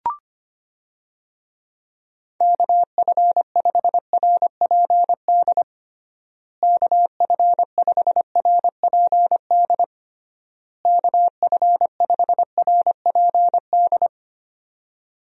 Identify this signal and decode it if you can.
Morse code, heard as a keyed tone, about 25 wpm, 700 Hz. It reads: KF5RPD KF5RPD KF5RPD
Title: USA Call Signs - 6 Characters - Repeated 3x - 25wpm